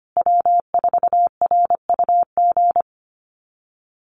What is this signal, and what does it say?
Morse code, heard as a keyed tone, about 25 wpm, 700 Hz. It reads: W4RUG
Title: USA Call Signs Repeated 2x 25wpm